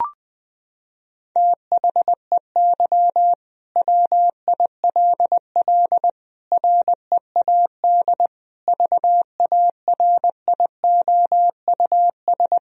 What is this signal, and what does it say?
Morse code, heard as a keyed tone, about 20 wpm, 700 Hz. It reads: THEY WILL READ VARIOUS
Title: Sentences from Top 1000 Words - Review Part 4 - 20wpm